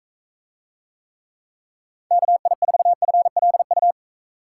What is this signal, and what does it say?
Morse code, heard as a keyed tone, about 42 wpm, 700 Hz. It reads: KI4FLU